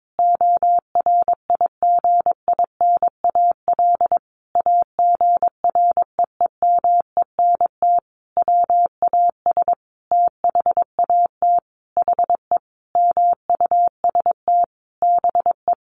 Morse code, heard as a keyed tone, about 22 wpm, 700 Hz. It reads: ORIGINAL AGREEMENT WAS THAT HE MUST BE
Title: Sentences from Top 1500 Words 22wpm